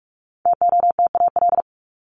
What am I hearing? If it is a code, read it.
Morse code, heard as a keyed tone, about 44 wpm, 700 Hz. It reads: TOTAL
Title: Top 1251-1500 Words 45wpm